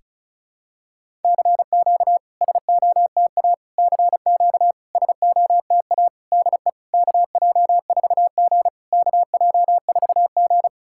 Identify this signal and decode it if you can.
Morse code, heard as a keyed tone, about 36 wpm, 700 Hz. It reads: CQ SOTA CQ SOTA DE KJ4G KJ4G
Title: SOTA QSO Practice 35wpm